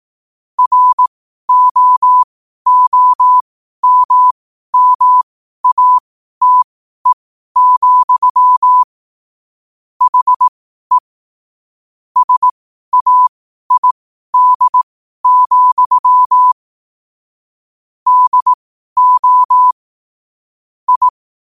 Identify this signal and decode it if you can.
Morse code, heard as a keyed tone, about 18 wpm, spaced as for 12 wpm, 1000 Hz. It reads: ROOMMATE, HE SAID, DO I
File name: qotdfiles / 12wpm